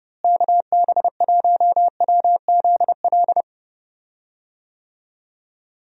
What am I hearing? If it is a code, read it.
Morse code, heard as a keyed tone, about 30 wpm, 700 Hz. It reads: KB1WZL
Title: USA Call Signs - 6 Characters - Repeated 2x - 30wpm